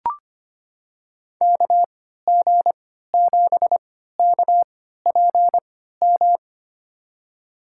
Morse code, heard as a keyed tone, about 25 wpm, spaced as for 14 wpm, 700 Hz. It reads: KG7KPM
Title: USA Call Signs - 6 Characters - 3x Character Spacing - 25wpm